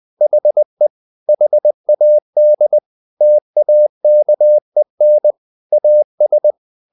Morse code, heard as a keyed tone, about 20 wpm, 600 Hz. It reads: HE HAD TAKEN AS
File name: the old man and the sea part 2 20wpm